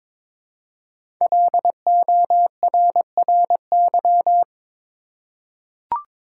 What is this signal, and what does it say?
Morse code, heard as a keyed tone, about 22 wpm, 700 Hz. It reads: LORRY
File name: 5 Letter Words - First 3 Letters Given - 22wpm